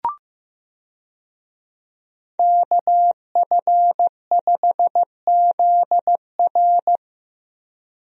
Morse code, heard as a keyed tone, about 15 wpm, 700 Hz. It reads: KF5ZR
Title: USA Call Signs - 5 Characters - Repeated 2x - 15wpm